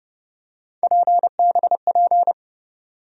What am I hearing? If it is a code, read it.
Morse code, heard as a keyed tone, about 30 wpm, 700 Hz. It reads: PBP
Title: USA Call Sign Postfixes - 3 Letters 30wpm